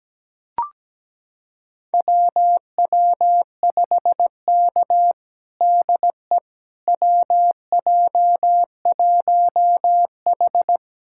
Morse code, heard as a keyed tone, about 17 wpm, 700 Hz. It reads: WW5K DE WJ1H
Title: USA Callsign DE Callsign - 4 Characters - 17wpm